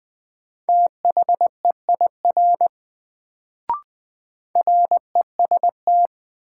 Morse code, heard as a keyed tone, about 20 wpm, 700 Hz. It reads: THEIR REST
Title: Sets of 2 Words - New Word Drill - 20wpm